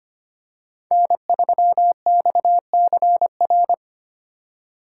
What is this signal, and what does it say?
Morse code, heard as a keyed tone, about 25 wpm, 700 Hz. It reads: N3XCR
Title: USA Call Signs - 5 Characters - 4x Character Spacing - 25wpm